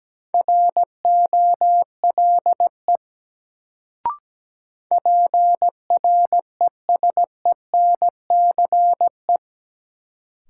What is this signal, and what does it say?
Morse code, heard as a keyed tone, about 17 wpm, 700 Hz. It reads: ROLE PRESENCE
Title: Sets of 3 Words - New Word Drill - 17wpm